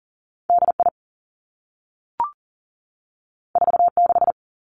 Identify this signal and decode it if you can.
Morse code, heard as a keyed tone, about 38 wpm, 700 Hz. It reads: D I 4 6
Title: Two Challenging Letter-Number Combinations 40wpm